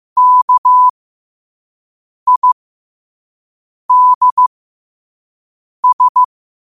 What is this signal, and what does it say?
Morse code, heard as a keyed tone, about 15 wpm, spaced as for 5 wpm, 1000 Hz. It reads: KIDS
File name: qotdfiles / 5wpm